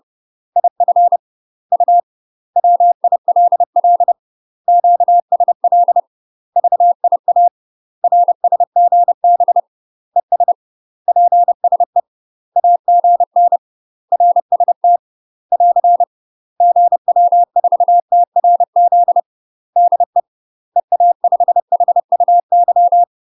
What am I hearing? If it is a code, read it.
Morse code, heard as a keyed tone, about 30 wpm, 700 Hz. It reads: IF U WILL QSL VIA RSGB ES PSE AGN RST <AR> GW4TRZ DE EA5HUY
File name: UK MASTER QSOs TRAINING / 30WPM / UK MASTER QSOs 0034